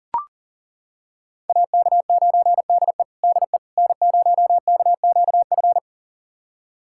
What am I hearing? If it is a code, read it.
Morse code, heard as a keyed tone, about 40 wpm, 700 Hz. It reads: AK9DE DE N0KQF
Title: USA Callsign DE Callsign - 5 Characters - 40wpm